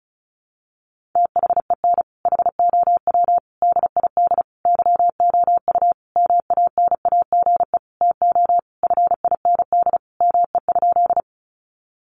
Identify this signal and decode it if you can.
Morse code, heard as a keyed tone, about 34 wpm, 700 Hz. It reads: THEN HOW DID YOU MANAGE TO FIND ME?
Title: Sentences from Top 1400 Words 35wpm